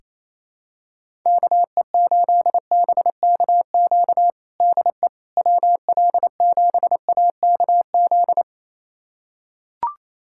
Morse code, heard as a keyed tone, about 28 wpm, 700 Hz. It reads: KE8BKQ DE WL7AKZ
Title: USA Callsign DE Callsign - 6 Characters - 28wpm